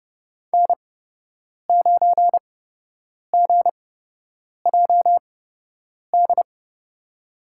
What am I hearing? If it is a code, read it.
Morse code, heard as a keyed tone, about 30 wpm, spaced as for 8 wpm, 700 Hz. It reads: N9GJD
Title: USA Call Signs - 5 Characters - 8x Character Spacing - 30wpm